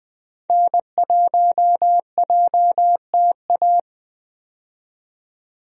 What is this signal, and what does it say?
Morse code, heard as a keyed tone, about 20 wpm, 700 Hz. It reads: N1JTA
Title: USA Call Signs - 5 Characters - 20wpm